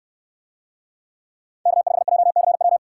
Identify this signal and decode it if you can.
Morse code, heard as a keyed tone, about 64 wpm, 700 Hz. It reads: K5JCW